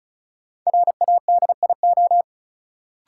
Morse code, heard as a keyed tone, about 36 wpm, 700 Hz. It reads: RADIO
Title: Sentences from Top 1000 Words - Review Part 3 - New Word Drill - 35wpm